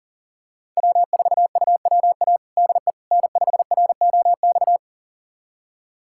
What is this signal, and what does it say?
Morse code, heard as a keyed tone, about 40 wpm, 700 Hz. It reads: W4UWA DE N5ROX
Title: USA Callsign DE Callsign - 5 Characters - 40wpm